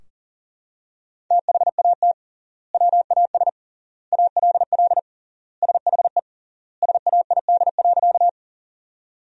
Morse code, heard as a keyed tone, about 40 wpm, 700 Hz. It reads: THAT WAS ALL SHE SAID.
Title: 3x Word-Spacing - Sentences from Top 100 Words - 40wpm